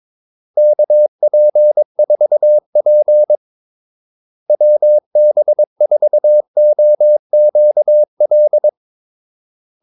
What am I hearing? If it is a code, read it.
Morse code, heard as a keyed tone, about 22 wpm, 600 Hz. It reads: KP4P WB4OQL
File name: Callsigns-22wpm